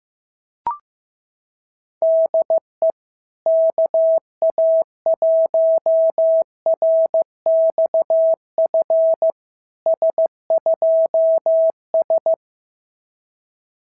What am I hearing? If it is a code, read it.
Morse code, heard as a keyed tone, about 14 wpm, 690 Hz. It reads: DE KA1RXF S2S
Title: SOTA QSO Practice 15wpm